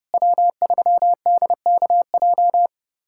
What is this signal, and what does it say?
Morse code, heard as a keyed tone, about 30 wpm, 700 Hz. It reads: W3DKJ
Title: USA Callsign DE Callsign - 5 Characters - 30wpm